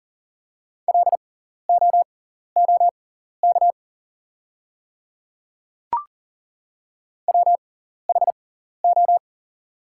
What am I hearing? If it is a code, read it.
Morse code, heard as a keyed tone, about 38 wpm, spaced as for 14 wpm, 700 Hz. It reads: LOOK WHO
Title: 6x Character Spacing - Top 100 Words - 40wpm